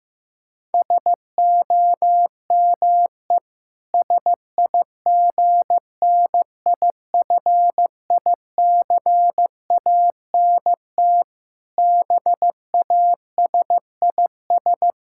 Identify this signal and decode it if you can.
Morse code, heard as a keyed tone, about 15 wpm, 700 Hz. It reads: SOME SIGNIFICANT BASIS
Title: Sentences from Top 1600 Words 15wpm